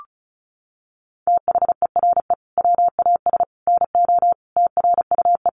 Morse code, heard as a keyed tone, about 34 wpm, 700 Hz. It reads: THERE WAS NO TRUE
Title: Sentences from Top 1300 Words 35wpm